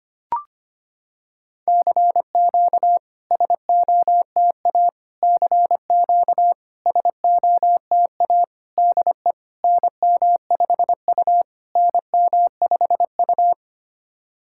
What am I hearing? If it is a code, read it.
Morse code, heard as a keyed tone, about 25 wpm, 700 Hz. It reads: CQ SOTA CQ SOTA DE NM5U NM5U
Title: SOTA QSO Practice 25wpm